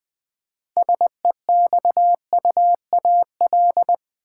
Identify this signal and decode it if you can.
Morse code, heard as a keyed tone, about 21 wpm, 700 Hz. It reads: SEXUAL
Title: Six Letter Words - 20wpm